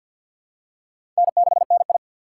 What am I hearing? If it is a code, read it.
Morse code, heard as a keyed tone, about 47 wpm, 700 Hz. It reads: N6NI
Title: USA Call Signs - 6x Character Spacing - 50wpm